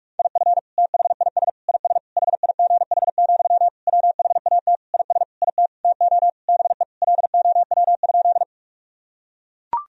Morse code, heard as a keyed tone, about 42 wpm, 700 Hz. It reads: IF THIS IS HIGH, WHAT IS IT TO BE LOW?
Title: Sentences from Top 1000 Words - Review Part 1 - 45wpm